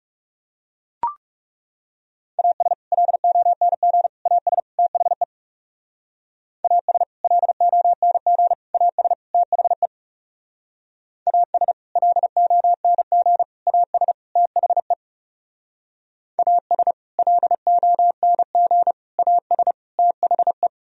Morse code, heard as a keyed tone, about 36 wpm, 700 Hz. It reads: AS LONG AS THE AS LONG AS THE AS LONG AS THE AS LONG AS THE
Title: Speed-Racing - Sets of 4 Words 30wpm